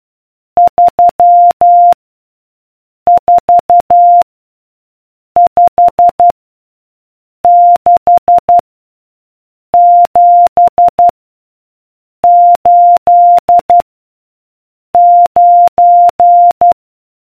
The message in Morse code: 3 4 5 6 7 8 9